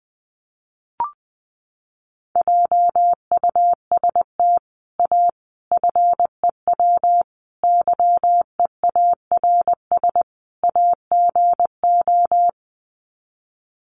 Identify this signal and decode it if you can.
Morse code, heard as a keyed tone, about 20 wpm, 700 Hz. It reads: JUST A FEW YEARS AGO
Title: Sets of 5 Words Encore 20wpm